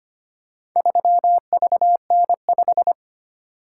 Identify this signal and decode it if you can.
Morse code, heard as a keyed tone, about 25 wpm, 700 Hz. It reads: 3VN5
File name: practice / 25wpm / Four Letter-Number Combinations - 25wpm